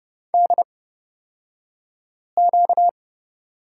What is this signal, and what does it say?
Morse code, heard as a keyed tone, about 31 wpm, spaced as for 5 wpm, 700 Hz. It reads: DQ
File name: Lesson 38d - Letter q - Two Characters 30wpm